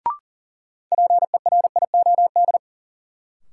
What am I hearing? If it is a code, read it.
Morse code, heard as a keyed tone, about 40 wpm, 700 Hz. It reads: PERIOD
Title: Sentences from Top 800 Words - New Word Drill - 40wpm